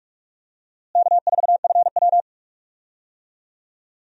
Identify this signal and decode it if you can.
Morse code, heard as a keyed tone, about 41 wpm, 700 Hz. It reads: K4FW
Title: USA Call Signs - 4 Characters - Repeated 2x - 45wpm